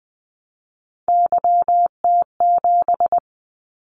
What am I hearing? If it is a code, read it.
Morse code, heard as a keyed tone, about 19 wpm, 700 Hz. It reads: YT7
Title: Callsign Prefixes for Top 25 DXCC - 20wpm